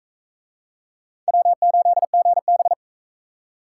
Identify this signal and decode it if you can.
Morse code, heard as a keyed tone, about 40 wpm, 700 Hz. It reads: W8GB